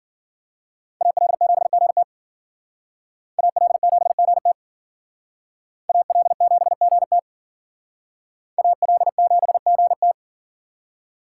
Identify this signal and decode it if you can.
Morse code, heard as a keyed tone, about 52 wpm, 700 Hz. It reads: AL7GT AL7GT AL7GT AL7GT